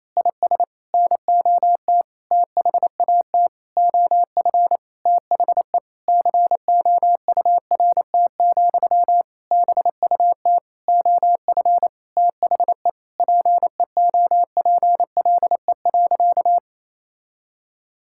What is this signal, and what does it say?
Morse code, heard as a keyed tone, about 28 wpm, 700 Hz. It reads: IS NOT THAT OF THE COURT, BUT OF THE PEOPLE.
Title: Sentences from Top 1600 Words 28wpm